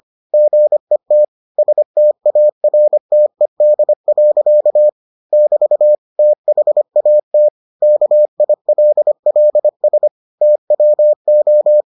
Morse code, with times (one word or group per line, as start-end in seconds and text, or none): 0.3-1.2 GET
1.6-4.9 STARTED.
5.3-6.0 <BT>
6.2-7.5 THAT
7.8-10.1 KILLS
10.4-11.8 TWO